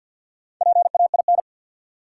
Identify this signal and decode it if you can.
Morse code, heard as a keyed tone, about 53 wpm, 700 Hz. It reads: PAIN
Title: Four Letter Words - 50wpm